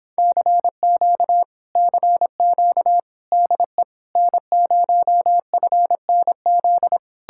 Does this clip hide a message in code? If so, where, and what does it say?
Yes: Morse code from 0.2 to 7.0 s, heard as a keyed tone, about 29 wpm, 700 Hz. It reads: CQ CQ DE N0FNZ